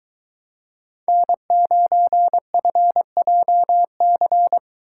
Morse code, heard as a keyed tone, about 21 wpm, 700 Hz. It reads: N9FJC